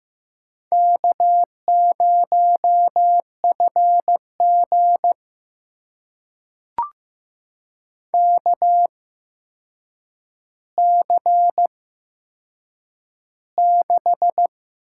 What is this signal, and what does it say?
Morse code, heard as a keyed tone, about 15 wpm, spaced as for 7 wpm, 700 Hz. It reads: K0FG KC6
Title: USA Call Signs - 8x Character Spacing - 15wpm